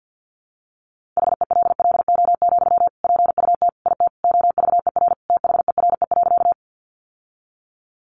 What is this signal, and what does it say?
Morse code, heard as a keyed tone, about 49 wpm, 700 Hz. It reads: HELLO, PUT IT OVER THERE.
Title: Sentences from Top 1100 Words 50wpm